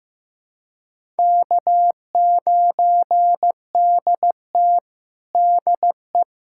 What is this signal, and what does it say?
Morse code, heard as a keyed tone, about 15 wpm, 700 Hz. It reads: K9DT DE
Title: USA Callsign DE Callsign - 4 Characters - 15wpm